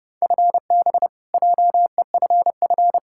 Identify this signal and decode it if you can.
Morse code, heard as a keyed tone, about 30 wpm, 700 Hz. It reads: FB JEFF
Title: Practice QSO - Exchange 3 - More Info - 30wpm